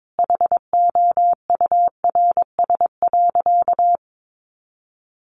Morse code, heard as a keyed tone, about 22 wpm, 700 Hz. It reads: HOURS.
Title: Additional Easy Phrases 22wpm